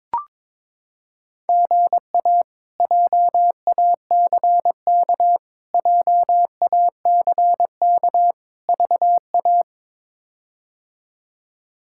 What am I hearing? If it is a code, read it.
Morse code, heard as a keyed tone, about 22 wpm, 700 Hz. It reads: GA JACK JACK VA
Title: K1USN SST QSO Practice - 22wpm